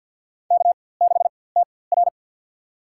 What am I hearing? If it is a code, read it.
Morse code, heard as a keyed tone, about 51 wpm, spaced as for 23 wpm, 700 Hz. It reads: K6TR
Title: USA Call Signs - 4x Character Spacing - 50wpm